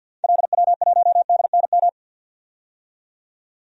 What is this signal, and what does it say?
Morse code, heard as a keyed tone, about 52 wpm, 700 Hz. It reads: LW1DNM
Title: DXCC-100 Call Signs - 6 Characters - 50wpm